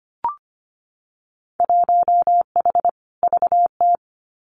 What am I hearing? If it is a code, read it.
Morse code, heard as a keyed tone, about 25 wpm, 700 Hz. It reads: 1H VT
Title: Winter Field Day Exchanges 25wpm